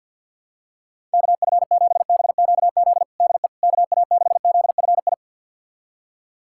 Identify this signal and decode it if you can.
Morse code, heard as a keyed tone, about 52 wpm, 700 Hz. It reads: KF7BQZ DE KA6ZFI
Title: USA Callsign DE Callsign - 6 Characters - 50wpm